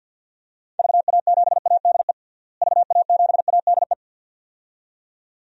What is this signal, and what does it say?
Morse code, heard as a keyed tone, about 46 wpm, 700 Hz. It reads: VA7ADE VA7ADE
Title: DXCC-100 Call Signs - 6 Characters - Repeated 2x - 50wpm